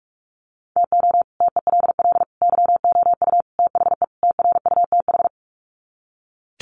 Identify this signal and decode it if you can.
Morse code, heard as a keyed tone, about 44 wpm, 700 Hz. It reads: TO TELL YOU THE TRUTH
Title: Sets of 5 Words 45wpm